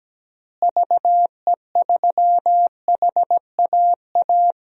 Morse code, heard as a keyed tone, about 17 wpm, 700 Hz. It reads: VE3HAA
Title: DXCC-100 Call Signs - 6 Characters - Repeated 2x - 17wpm